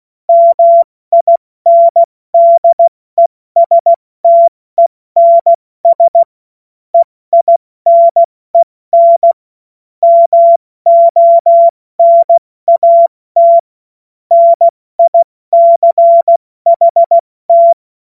Morse code, heard as a keyed tone, about 16 wpm, 680 Hz. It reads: MINDESTENS EINEN MONAT NICHT
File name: Morsekurs / Witze / Witz 10. Mai 2022 70 F80